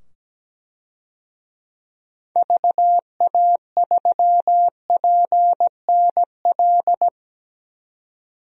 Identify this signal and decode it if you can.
Morse code, heard as a keyed tone, about 17 wpm, 700 Hz. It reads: VA3PNL
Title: DXCC-100 Call Signs - 6 Characters - 17wpm